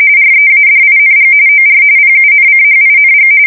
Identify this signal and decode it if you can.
RTTY, about 45.45 baud. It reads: READING OR PUNCHING
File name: fsk170 / teletype-corp